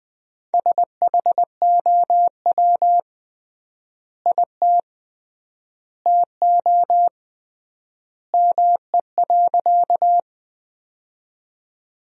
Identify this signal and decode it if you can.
Morse code, heard as a keyed tone, about 20 wpm, 700 Hz. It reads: SHOW IT TO ME.
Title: 3x Word-Spacing - Sentences from Top 200 Words - 20wpm